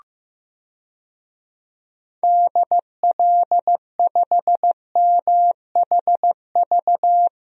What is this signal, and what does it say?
Morse code, heard as a keyed tone, about 15 wpm, 700 Hz. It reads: DL5MHV
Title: DXCC-100 Call Signs - 6 Characters - Repeated 2x - 15wpm